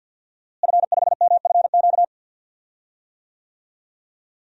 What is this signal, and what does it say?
Morse code, heard as a keyed tone, about 51 wpm, 700 Hz. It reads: F5MFQ
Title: DXCC-100 Call Signs - 5 Characters - Repeated 2x - 50wpm